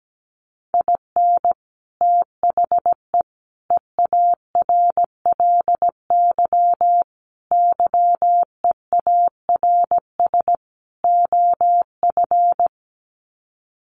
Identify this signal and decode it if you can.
Morse code, heard as a keyed tone, about 17 wpm, 690 Hz. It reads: IN THE EARLY YEARS OF
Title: Sets of 5 Words Encore 17wpm